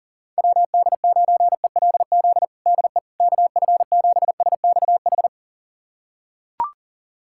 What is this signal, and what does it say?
Morse code, heard as a keyed tone, about 40 wpm, 690 Hz. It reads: WD9ELZ DE KF7SXH
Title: USA Callsign DE Callsign - 6 Characters - 40wpm